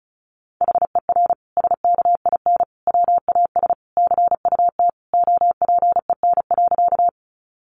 Morse code, heard as a keyed tone, about 34 wpm, 700 Hz. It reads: HER SKIN WAS CUT OPEN.
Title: Sentences from Top 1300 Words 35wpm